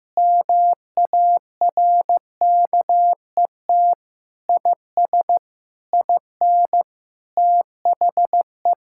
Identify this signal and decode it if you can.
Morse code, heard as a keyed tone, about 15 wpm, 700 Hz. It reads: MARKET IS IN THE
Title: Sentences from Top 1000 Words - Review Part 6 - 15wpm